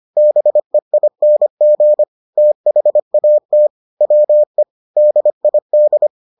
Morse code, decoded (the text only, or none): BEING THAT WE DID